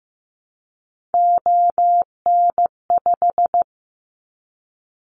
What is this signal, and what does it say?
Morse code, heard as a keyed tone, about 15 wpm, 700 Hz. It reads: ON5
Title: Callsign Prefixes for Top 25 DXCC - 15wpm